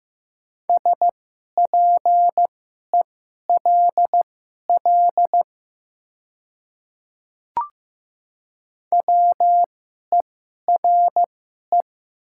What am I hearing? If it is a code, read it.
Morse code, heard as a keyed tone, about 15 wpm, spaced as for 11 wpm, 690 Hz. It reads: SPELL WERE
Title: 2x Character Spacing - Top 200 Words - 15wpm